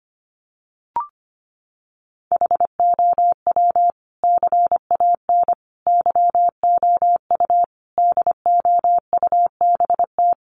Morse code, heard as a keyed tone, about 25 wpm, 700 Hz. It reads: HOW CAN YOU DOUBT
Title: Sentences from Top 1500 Words 25wpm